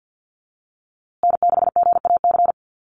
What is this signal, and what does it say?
Morse code, heard as a keyed tone, about 48 wpm, 700 Hz. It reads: N6GAC
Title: USA Call Signs Repeated 3x 50wpm